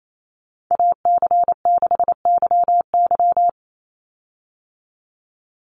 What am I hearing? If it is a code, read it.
Morse code, heard as a keyed tone, about 28 wpm, 700 Hz. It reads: AC6YY
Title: Call Signs Repeated 2x - 28wpm